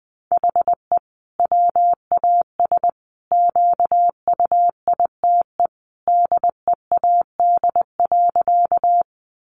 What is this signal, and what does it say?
Morse code, heard as a keyed tone, about 20 wpm, 700 Hz. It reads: HE WAS QUITE DEAD.
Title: Sentences from Top 1500 Words 20wpm